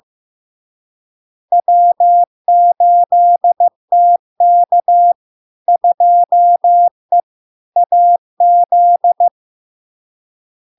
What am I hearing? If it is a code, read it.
Morse code, heard as a keyed tone, about 15 wpm, 700 Hz. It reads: W8TK 2E AZ
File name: LARK-FD-2022-Log-in-CW-15wpm-700-Hz